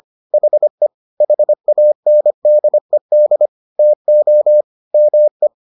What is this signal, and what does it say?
Morse code, heard as a keyed tone, about 25 wpm, 600 Hz. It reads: HE HANDED TO ME